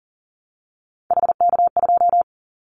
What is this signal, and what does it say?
Morse code, heard as a keyed tone, about 41 wpm, 700 Hz. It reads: HK2
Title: Callsign Prefixes for Top 100 DXCC - 40wpm